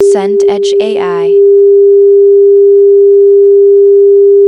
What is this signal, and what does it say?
Dial tone (Japan), 400 Hz, with speech.
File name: sfx / preview / 745 phone dial tone